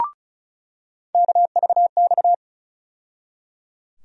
Morse code, heard as a keyed tone, about 36 wpm, 700 Hz. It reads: KVX
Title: USA Call Sign Postfixes - 3 Letters 35wpm